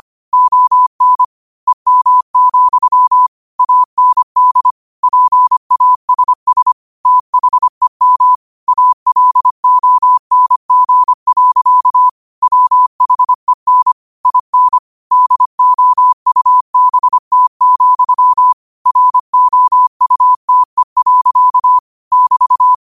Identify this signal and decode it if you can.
Morse code, heard as a keyed tone, about 25 wpm, 1000 Hz. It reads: ON EM, AND PASS THEM ALONG. WHEN IN DOUBT, ROUTE. <BT>
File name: qotdfiles / 25wpm